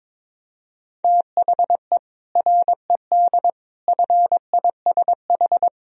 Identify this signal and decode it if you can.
Morse code, heard as a keyed tone, about 21 wpm, 700 Hz. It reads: THE RED FISH
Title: Sentences from Top 1000 Words - Review Part 2 - 22wpm